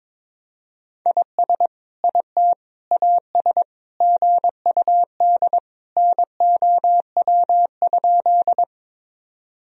Morse code, heard as a keyed tone, about 22 wpm, 700 Hz. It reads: IS IT AS GUD NOW?
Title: Sentences from Top 200 Words and Abbreviations 22wpm